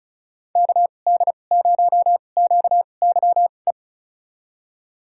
Morse code, heard as a keyed tone, about 36 wpm, spaced as for 25 wpm, 700 Hz. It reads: KD0QYE
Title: USA Call Signs - 6 Characters - 2x Character Spacing - 35wpm